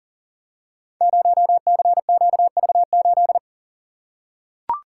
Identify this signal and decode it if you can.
Morse code, heard as a keyed tone, about 41 wpm, 690 Hz. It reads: 0CQV8
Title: Code Groups - 5 Letter:Numbers - 40wpm